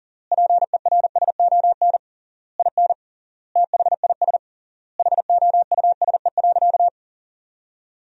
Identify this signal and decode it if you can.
Morse code, heard as a keyed tone, about 40 wpm, 700 Hz. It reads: PERSON IN THIS HOUSE.
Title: 3x Word-Spacing - Sentences from Top 400 Words - 40wpm